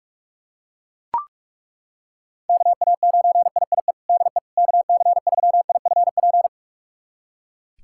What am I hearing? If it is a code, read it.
Morse code, heard as a keyed tone, about 43 wpm, 700 Hz. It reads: KA9IIE DE KC3IFP
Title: USA Callsign DE Callsign 45wpm